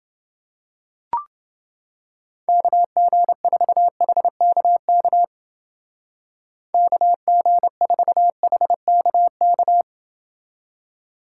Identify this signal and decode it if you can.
Morse code, heard as a keyed tone, about 28 wpm, 700 Hz. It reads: KG4HKK KG4HKK